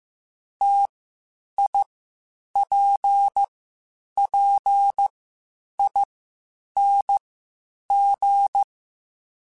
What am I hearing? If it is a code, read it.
Morse code, heard as a keyed tone, about 15 wpm, spaced as for 8 wpm, 800 Hz. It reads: TIPPING
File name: Difficult Combos / Difficult Combos-mp3s / T AOEI @13